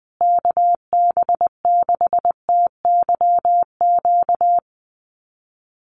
Morse code, heard as a keyed tone, about 20 wpm, 700 Hz. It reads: KB6TYQ
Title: USA Call Signs Repeated 2x 20wpm